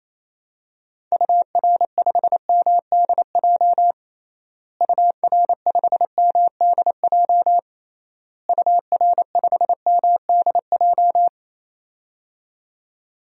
Morse code, heard as a keyed tone, about 28 wpm, 700 Hz. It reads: UR5MDJ UR5MDJ UR5MDJ
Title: DXCC-100 Call Signs - 6 Characters - Repeated 3x - 28wpm